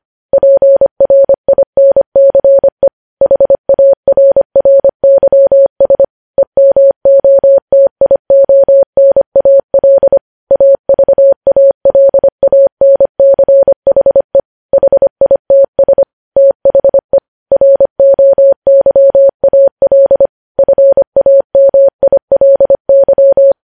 Morse code, heard as a keyed tone, about 25 wpm, 550 Hz. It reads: PRINCE HARRYS EMOTIONAL AVALANCHE HITS THE ROYAL FAMILY
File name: podcast / News headlines-1x-25x25